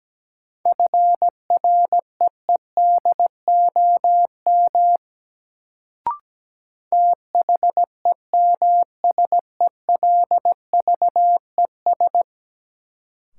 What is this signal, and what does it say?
Morse code, heard as a keyed tone, about 17 wpm, 700 Hz. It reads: FREEDOM THEMSELVES
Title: Sentences from Top 800 Words - New Word Drill - 17wpm